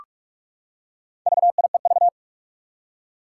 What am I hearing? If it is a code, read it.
Morse code, heard as a keyed tone, about 44 wpm, 700 Hz. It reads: VIEV
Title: Four Challenging Letter-Number Combinations 45wpm